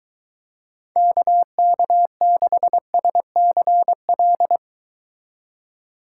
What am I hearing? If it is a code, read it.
Morse code, heard as a keyed tone, about 23 wpm, 700 Hz. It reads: KK6SCL